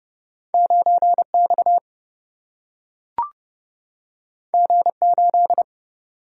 Morse code, heard as a keyed tone, about 30 wpm, 700 Hz. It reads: 9X G8
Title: Single-Letter Single-Number 30wpm